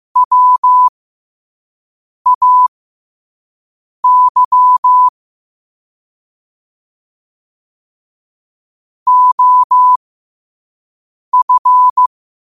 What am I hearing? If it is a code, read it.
Morse code, heard as a keyed tone, about 15 wpm, spaced as for 5 wpm, 1000 Hz. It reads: WAY OF
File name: qotdfiles / 5wpm